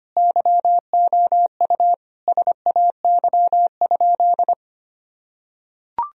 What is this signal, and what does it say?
Morse code, heard as a keyed tone, about 25 wpm, 700 Hz. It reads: YOU SAY?
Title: Sentences from Top 1000 Words - Review Part 1 - 25wpm